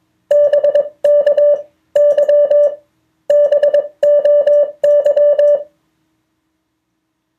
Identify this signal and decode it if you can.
Morse code, heard as a keyed tone, about 21 wpm, spaced as for 17 wpm, 580 Hz. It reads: BKY BOY
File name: 2of4-words-1